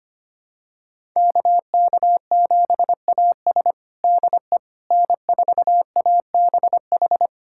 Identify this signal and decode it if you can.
Morse code, heard as a keyed tone, about 25 wpm, 700 Hz. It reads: KK7AS DE N4ABH
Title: USA Callsign DE Callsign - 5 Characters - 25wpm